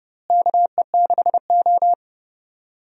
Morse code, heard as a keyed tone, about 30 wpm, 700 Hz. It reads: KE6O
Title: USA Call Signs - 4 Characters - Repeated 3x - 30wpm